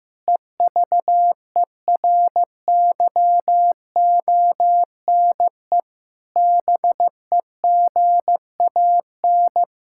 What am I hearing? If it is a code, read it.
Morse code, heard as a keyed tone, about 16 wpm, 700 Hz. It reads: EVERYONE BEGAN
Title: Sentences from Top 1600 Words 15wpm